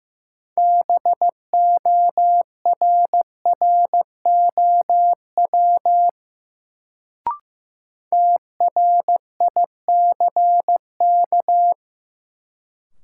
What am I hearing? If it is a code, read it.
Morse code, heard as a keyed tone, about 15 wpm, 700 Hz. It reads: BORROW TRICK
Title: Sentences from Top 1500 Words - New Word Drill Alt - 15wpm